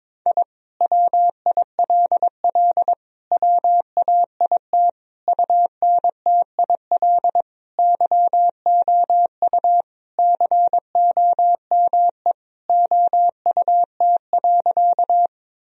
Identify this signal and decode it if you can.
Morse code, heard as a keyed tone, about 22 wpm, 700 Hz. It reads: I WILL WAIT UNTIL YOU COME OUT.
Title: Sentences from Top 1000 Words - Review Part 1 - 22wpm